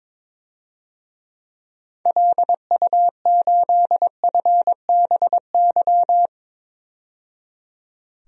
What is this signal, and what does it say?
Morse code, heard as a keyed tone, about 22 wpm, 700 Hz. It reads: LU8FBY
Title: DXCC-100 Call Signs - 6 Characters - 22wpm